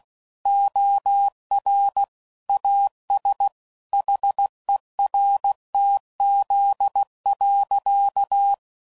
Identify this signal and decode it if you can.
Morse code, heard as a keyed tone, about 16 wpm, 800 Hz. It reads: OR AS HERTZ.
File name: CW Code Groups @15WPM / Technical Subjects / Reactance